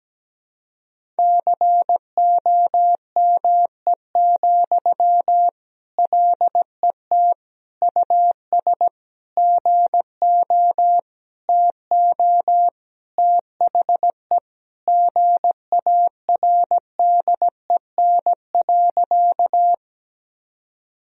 Morse code, heard as a keyed tone, about 17 wpm, 700 Hz. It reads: COME, LET US GO TO THE GARDEN.
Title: Sentences from Top 1000 Words - Review Part 4 - 17wpm